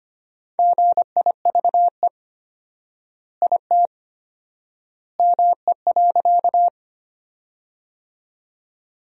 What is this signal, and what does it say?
Morse code, heard as a keyed tone, about 25 wpm, 700 Hz. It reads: GIVE IT ME.
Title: 4x Word-Spacing - Sentences from Top 200 Words - 25wpm